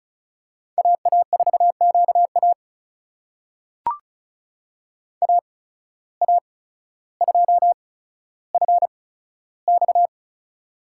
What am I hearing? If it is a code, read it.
Morse code, heard as a keyed tone, about 34 wpm, spaced as for 15 wpm, 700 Hz. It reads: AA4QA AA2FX
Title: USA Call Signs - 5 Characters - 8x Character Spacing - 35wpm